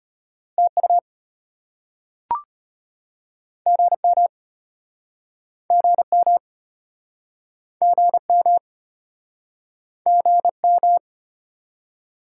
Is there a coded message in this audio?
Yes: Morse code from 0.6 to 11.0 s, heard as a keyed tone, about 34 wpm, 700 Hz. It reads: TU GM GM GM GM